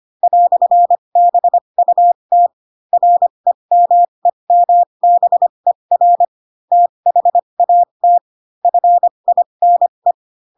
Morse code, heard as a keyed tone, about 25 wpm, spaced as for 22 wpm, 700 Hz. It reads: "BUT REMEMBER THAT FINE